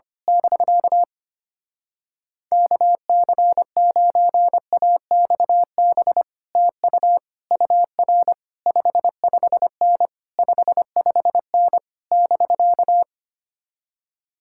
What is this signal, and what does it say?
Morse code, heard as a keyed tone, about 24 wpm, 700 Hz. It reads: <BK> KC9AXB TU UR 55N 55N <BK>
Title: Speed-Racing Common QSO Phrases - 25wpm